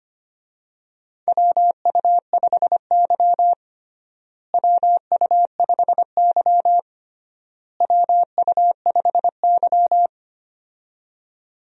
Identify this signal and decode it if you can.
Morse code, heard as a keyed tone, about 25 wpm, 700 Hz. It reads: WU5Y WU5Y WU5Y
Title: USA Call Signs - 4 Characters - Repeated 3x - 25wpm